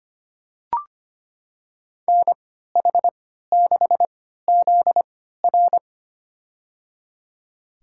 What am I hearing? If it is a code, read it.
Morse code, heard as a keyed tone, about 25 wpm, spaced as for 14 wpm, 700 Hz. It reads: NH6ZR
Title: USA Call Signs - 5 Characters - 3x Character Spacing - 25wpm